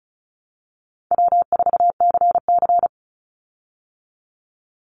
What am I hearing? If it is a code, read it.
Morse code, heard as a keyed tone, about 34 wpm, 700 Hz. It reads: W4CC
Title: Easy Call Signs Repeated 2x - 35wpm